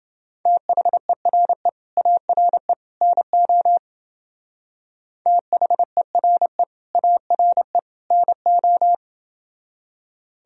Morse code, heard as a keyed tone, about 29 wpm, 700 Hz. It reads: THERE ARE NO THERE ARE NO